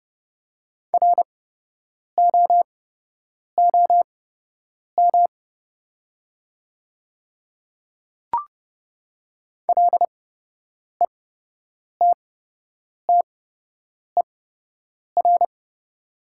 Morse code, heard as a keyed tone, about 30 wpm, spaced as for 8 wpm, 700 Hz. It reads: ROOM LETTER
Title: 8x Character Spacing - Top 300 Words - 30wpm